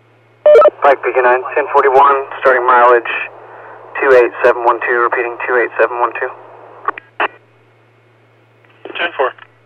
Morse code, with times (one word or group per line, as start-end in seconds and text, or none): none